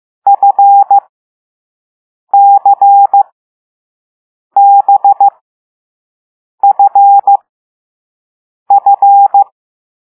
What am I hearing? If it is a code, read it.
Morse code, heard as a keyed tone, about 15 wpm, spaced as for 5 wpm, 800 Hz. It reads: FCBFF
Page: Play Lesson 6 C - all letters in previous lessons